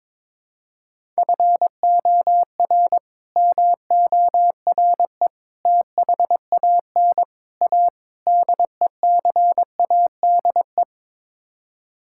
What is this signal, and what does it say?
Morse code, heard as a keyed tone, about 22 wpm, 700 Hz. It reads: FOR MORE THAN A DECADE
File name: practice / 22wpm / Build A Phrase - 5-Word Phrases - 22wpm